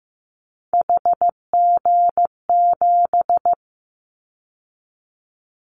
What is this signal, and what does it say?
Morse code, heard as a keyed tone, about 15 wpm, 700 Hz. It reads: HG7
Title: Callsign Prefixes for Top 50 DXCC - 15wpm